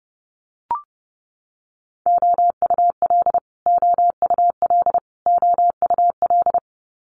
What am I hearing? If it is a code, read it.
Morse code, heard as a keyed tone, about 30 wpm, 700 Hz. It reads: OUL OUL OUL
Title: Multi-letter Phonemes Repeated 3x 30wpm